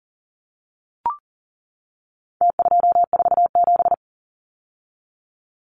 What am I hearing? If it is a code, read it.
Morse code, heard as a keyed tone, about 40 wpm, 700 Hz. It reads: T247
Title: Numbers - Contest Style - 4-Digit 40wpm